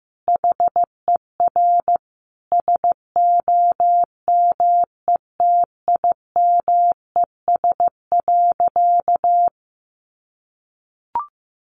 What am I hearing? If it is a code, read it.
Morse code, heard as a keyed tone, about 15 wpm, 700 Hz. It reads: HER SOMETIMES.
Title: Sentences from Top 600 Words 15wpm